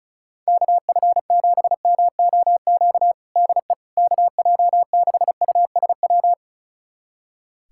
Morse code, heard as a keyed tone, about 36 wpm, 700 Hz. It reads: KF7MOQ DE KJ6USW
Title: USA Callsign DE Callsign - 6 Characters - 35wpm